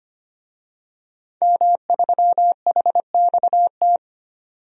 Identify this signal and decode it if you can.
Morse code, heard as a keyed tone, about 25 wpm, 700 Hz. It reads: M3HXT
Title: DXCC-100 Call Signs - 5 Characters - 25wpm